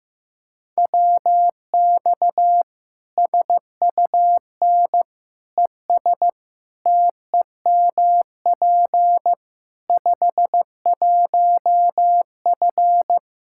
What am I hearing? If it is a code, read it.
Morse code, heard as a keyed tone, about 15 wpm, 700 Hz. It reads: WX SUN ES TEMP 51F
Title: Practice QSO - Exchange 2 - Info - 15wpm